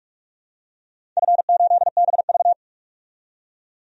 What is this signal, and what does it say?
Morse code, heard as a keyed tone, about 46 wpm, 700 Hz. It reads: F8BV
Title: DXCC-100 Call Signs - 4 Characters - 45wpm